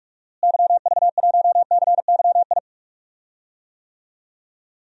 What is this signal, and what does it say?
Morse code, heard as a keyed tone, about 46 wpm, 700 Hz. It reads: YV1CYI
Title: DXCC-100 Call Signs - 6 Characters - Repeated 2x - 45wpm